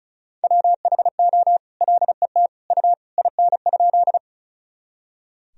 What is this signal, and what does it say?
Morse code, heard as a keyed tone, about 34 wpm, 700 Hz. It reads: WHO LET U IN?
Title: Sentences from Top 300 Words and Abbreviations 35wpm